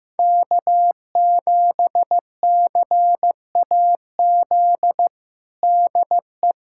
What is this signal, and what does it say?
Morse code, heard as a keyed tone, about 15 wpm, 700 Hz. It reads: K7CAZ DE
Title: USA Callsign DE Callsign - 5 Characters - 15wpm